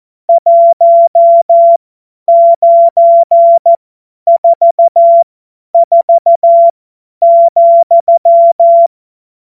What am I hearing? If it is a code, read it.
Morse code, heard as a keyed tone, about 14 wpm, spaced as for 10 wpm, 680 Hz. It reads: 1944,